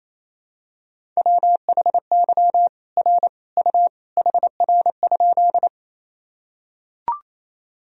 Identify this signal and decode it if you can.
Morse code, heard as a keyed tone, about 28 wpm, 700 Hz. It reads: WHY R U HR?
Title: Sentences from Top 200 Words and Abbreviations 28wpm